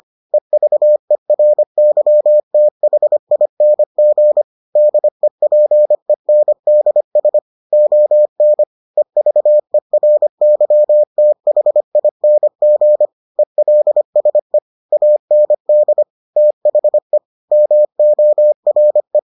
Morse code, heard as a keyed tone, about 25 wpm, 600 Hz. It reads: EVERYTHING DEPENDS ON EVERYTHING ELSE AND THE MORE